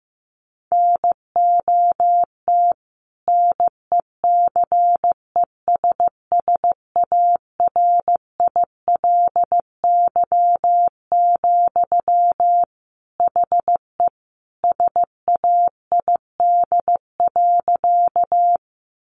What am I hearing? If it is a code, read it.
Morse code, heard as a keyed tone, about 15 wpm, 700 Hz. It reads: NOT NECESSARILY, HE SAID.
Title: Sentences from Top 1400 Words 15wpm